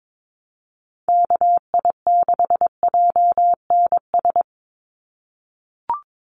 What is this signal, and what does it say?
Morse code, heard as a keyed tone, about 22 wpm, 700 Hz. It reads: KI6JNS
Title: USA Call Signs 22wpm